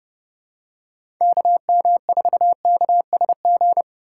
Morse code, heard as a keyed tone, about 30 wpm, 700 Hz. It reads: KM4KSG
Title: USA Call Signs - 6 Characters - 3x Character Spacing - 30wpm